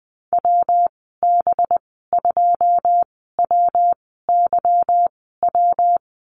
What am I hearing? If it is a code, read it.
Morse code, heard as a keyed tone, about 20 wpm, spaced as for 15 wpm, 700 Hz. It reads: WB2WYW
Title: 193 Lesson 31g - Letter v - Call Signs 20wpm